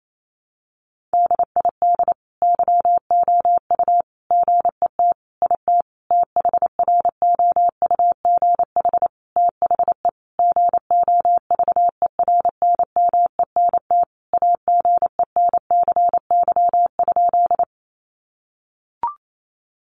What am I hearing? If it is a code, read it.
Morse code, heard as a keyed tone, about 28 wpm, 700 Hz. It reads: DID YOU GET IT THROUGH THE GOVERNMENT AGENCY?
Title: Sentences from Top 1300 Words 28wpm